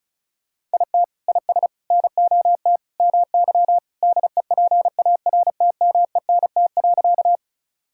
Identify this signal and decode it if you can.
Morse code, heard as a keyed tone, about 36 wpm, 700 Hz. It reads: IT IS NOT MY DEPARTMENT.
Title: Sentences from Top 1000 Words - Review Part 5 - 35wpm